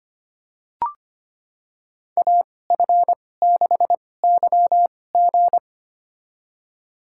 Morse code, heard as a keyed tone, about 25 wpm, spaced as for 18 wpm, 700 Hz. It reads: AF6YG
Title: USA Call Signs - 5 Characters - 2x Character Spacing - 25wpm